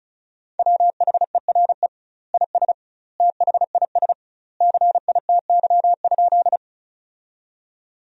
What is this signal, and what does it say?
Morse code, heard as a keyed tone, about 36 wpm, 700 Hz. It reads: WHERE IS THIS CITY?
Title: 2x Word-Spacing - Sentences from Top 300 Words - 35wpm